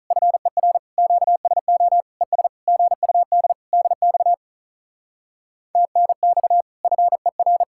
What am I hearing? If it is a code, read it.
Morse code, heard as a keyed tone, about 46 wpm, 700 Hz. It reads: FER QSO ES GUD DX TNX FER